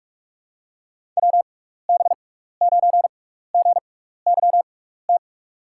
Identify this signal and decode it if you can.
Morse code, heard as a keyed tone, about 44 wpm, spaced as for 15 wpm, 700 Hz. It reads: WB9GYT
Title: USA Call Signs - 6 Characters - 6x Character Spacing - 45wpm